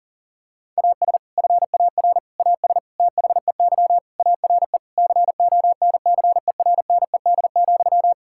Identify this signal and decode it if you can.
Morse code, heard as a keyed tone, about 40 wpm, 700 Hz. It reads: AS FAR AS THEY ARE CONCERNED,
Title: Sentences from Top 1000 Words - Review Part 6 - 40wpm